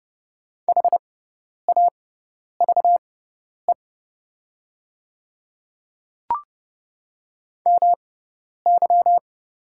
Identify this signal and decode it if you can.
Morse code, heard as a keyed tone, about 29 wpm, spaced as for 10 wpm, 700 Hz. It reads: HAVE MY
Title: 6x Character Spacing - Top 100 Words - 30wpm